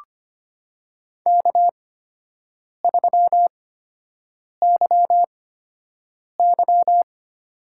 Morse code, heard as a keyed tone, about 25 wpm, spaced as for 7 wpm, 700 Hz. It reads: K3YY
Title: USA Call Signs - 4 Characters - 8x Character Spacing - 25wpm